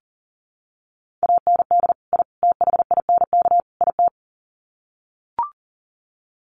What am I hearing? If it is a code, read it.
Morse code, heard as a keyed tone, about 40 wpm, 700 Hz. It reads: AND I THINK IT
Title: Sets of 4 Words 40wpm